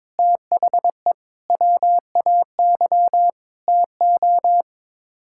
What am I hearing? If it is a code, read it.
Morse code, heard as a keyed tone, about 22 wpm, 700 Hz. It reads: THE WAY TO
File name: practice / 22wpm / Sets of 8 Words - 22wpm